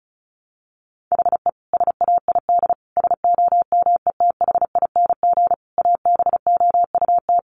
Morse code, heard as a keyed tone, about 34 wpm, 700 Hz. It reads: HE SAID SOMETHING ABOUT
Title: Sentences from Top 1500 Words 35wpm